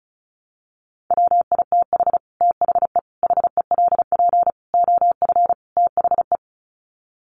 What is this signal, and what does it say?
Morse code, heard as a keyed tone, about 34 wpm, 700 Hz. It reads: WITH THE HELP OF THE
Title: Sets of 5 Words Encore 35wpm